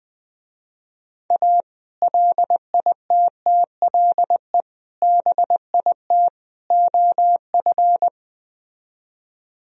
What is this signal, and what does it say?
Morse code, heard as a keyed tone, about 20 wpm, 700 Hz. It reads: A LITTLE BIT OF
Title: Speed-Racing - Sets of 4 Words 20wpm